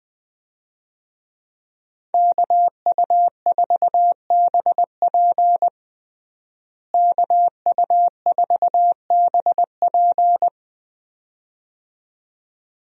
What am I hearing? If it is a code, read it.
Morse code, heard as a keyed tone, about 20 wpm, 700 Hz. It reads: KU4BP KU4BP
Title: USA Call Signs - 5 Characters - Repeated 2x - 20wpm